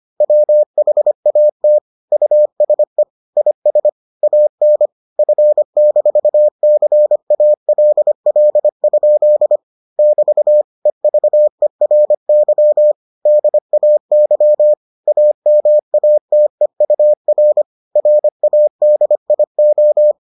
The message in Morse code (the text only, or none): WHAT USE IS AN F-CALL? <BT> EVERY DAY AMATEUR RADIO